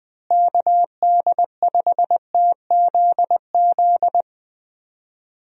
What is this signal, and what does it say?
Morse code, heard as a keyed tone, about 20 wpm, 700 Hz. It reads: KD5TZZ
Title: USA Call Signs - 6 Characters - Repeated 2x - 20wpm